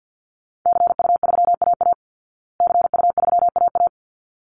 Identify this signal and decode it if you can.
Morse code, heard as a keyed tone, about 50 wpm, 700 Hz. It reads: CU3AA CU3AA
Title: Call Signs Repeated 3x - 50wpm